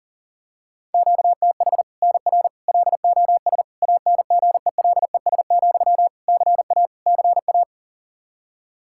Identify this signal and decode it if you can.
Morse code, heard as a keyed tone, about 40 wpm, 700 Hz. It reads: QTH NR LOS ANGELES, CA CA
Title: Practice QSO - Exchange 1 - Beginning - 40wpm